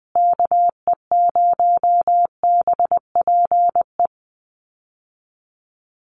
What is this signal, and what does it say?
Morse code, heard as a keyed tone, about 20 wpm, 700 Hz. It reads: KE0BPE
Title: USA Call Signs Repeated 3x 20wpm